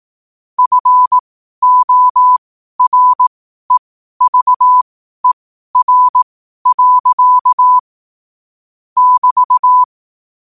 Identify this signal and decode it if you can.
Morse code, heard as a keyed tone, about 18 wpm, spaced as for 12 wpm, 1000 Hz. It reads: FOREVER. <BT>